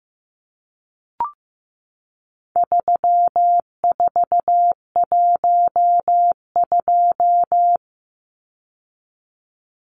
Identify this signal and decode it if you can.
Morse code, heard as a keyed tone, about 15 wpm, 700 Hz. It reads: 3412
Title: Numbers - 4-Digit 15wpm